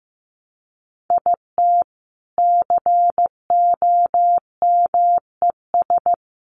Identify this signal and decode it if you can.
Morse code, heard as a keyed tone, about 15 wpm, 700 Hz. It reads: IT COMES
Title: Sets of 3 Words 15wpm